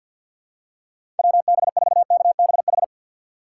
Morse code, heard as a keyed tone, about 52 wpm, 700 Hz. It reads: WB4KBH
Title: USA Call Signs - 2x Character Spacing - 50wpm